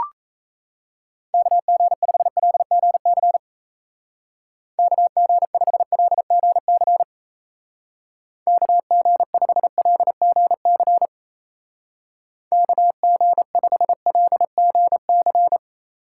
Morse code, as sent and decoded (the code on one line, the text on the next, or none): -.- --. ..... .-.. --. -.-. / -.- --. ..... .-.. --. -.-. / -.- --. ..... .-.. --. -.-. / -.- --. ..... .-.. --. -.-.
KG5LGC KG5LGC KG5LGC KG5LGC